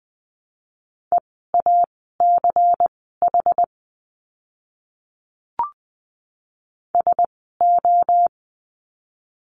Morse code, heard as a keyed tone, about 20 wpm, spaced as for 14 wpm, 700 Hz. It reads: EACH SO
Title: 230 Lesson 37f - Letter x - Words 20wpm